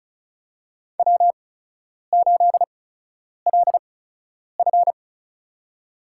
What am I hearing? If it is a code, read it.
Morse code, heard as a keyed tone, about 34 wpm, spaced as for 10 wpm, 700 Hz. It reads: W8LF
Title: USA Call Signs - 4 Characters - 8x Character Spacing - 35wpm